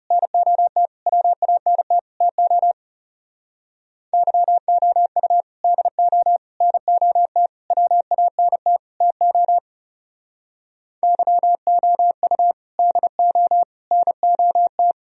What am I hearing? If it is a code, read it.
Morse code, heard as a keyed tone, about 36 wpm, 700 Hz. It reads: NOT WANT TO YOU DO NOT WANT TO YOU DO NOT
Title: Speed-Racing - Sets of 5 Words 30wpm